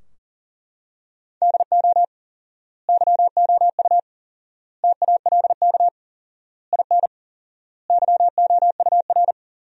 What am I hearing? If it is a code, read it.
Morse code, heard as a keyed tone, about 40 wpm, 700 Hz. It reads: DO YOU TALK IN YOUR
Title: 4x Word-Spacing - Sentences from Top 500 Words - 40wpm